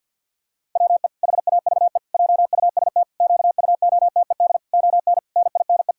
Morse code, heard as a keyed tone, about 51 wpm, 700 Hz. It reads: WE HAVE JUST QUOTED ON NINE
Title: Pangram Challenge v2 - 50wpm